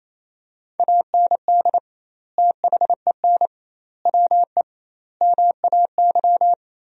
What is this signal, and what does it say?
Morse code, heard as a keyed tone, about 28 wpm, 700 Hz. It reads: AND THEN WE MAY
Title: 2x Word-Spacing - Sentences from Top 300 Words - 28wpm